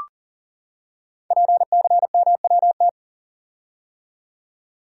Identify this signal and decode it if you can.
Morse code, heard as a keyed tone, about 42 wpm, 690 Hz. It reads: PCMWT
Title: Code Groups - 5 Letters - 40wpm